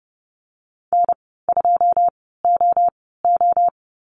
Morse code, heard as a keyed tone, about 30 wpm, spaced as for 17 wpm, 700 Hz. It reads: N2OO
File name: Lesson 15g - Number 5 - Call Signs 30wpm